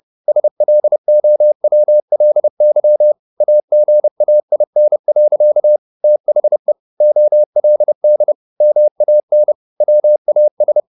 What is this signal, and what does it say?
Morse code, heard as a keyed tone, about 30 wpm, 600 Hz. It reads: SLOWLY AGAIN. THE OLD MAN WAS